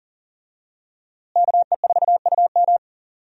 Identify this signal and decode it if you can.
Morse code, heard as a keyed tone, about 40 wpm, 700 Hz. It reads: KE4UM
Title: USA Call Signs - 5 Characters - Repeated 2x - 40wpm